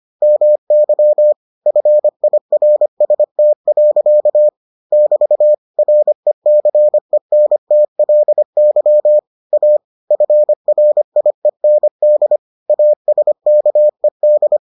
Morse code, heard as a keyed tone, about 25 wpm, 600 Hz. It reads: MY FIRST. <BT> RECENTLY A FRIEND ASKED